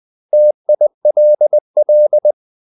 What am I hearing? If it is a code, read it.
Morse code, heard as a keyed tone, about 20 wpm, 600 Hz. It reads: TILL
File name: CW / Practice / 500MostCommonEnglishWords20